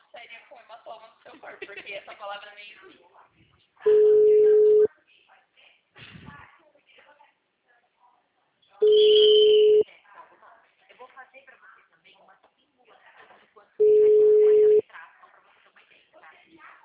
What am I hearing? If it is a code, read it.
Ringback tone (Europe, 425 Hz), with speech.